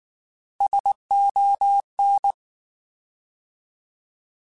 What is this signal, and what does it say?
Morse code, heard as a keyed tone, about 19 wpm, 800 Hz. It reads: SON
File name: Alphabetical / Alphabetical mp3s / Ss @18